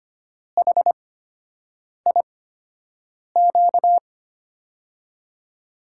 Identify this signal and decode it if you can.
Morse code, heard as a keyed tone, about 25 wpm, spaced as for 7 wpm, 700 Hz. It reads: HIQ
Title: USA Call Signs - 8x Character Spacing - 25wpm